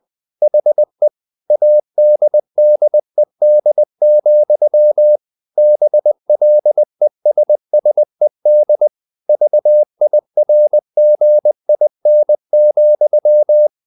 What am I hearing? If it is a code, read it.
Morse code, heard as a keyed tone, about 20 wpm, 600 Hz. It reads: HE ADDED, BLESSED VIRGIN,
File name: the old man and the sea / the old man and the sea part 1 20wpm